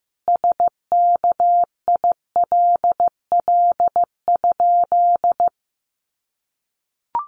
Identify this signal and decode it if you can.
Morse code, heard as a keyed tone, about 15 wpm, 700 Hz. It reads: SKILL?
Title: Sentences from Top 1200 Words 15wpm